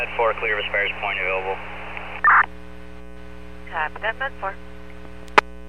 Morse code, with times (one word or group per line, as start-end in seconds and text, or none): none